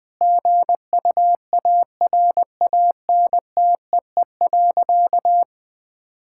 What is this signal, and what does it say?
Morse code, heard as a keyed tone, about 21 wpm, 700 Hz. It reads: GUARANTEE.
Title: Sentences from Top 1600 Words 20wpm